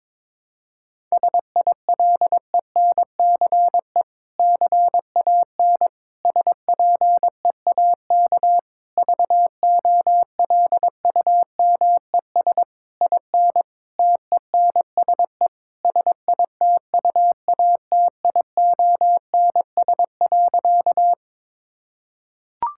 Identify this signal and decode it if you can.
Morse code, heard as a keyed tone, about 22 wpm, 700 Hz. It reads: SILENCE CAN SPEAK VOLUMES IN TENSE SITUATIONS.
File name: Difficult Sentences with Hints - 22wpm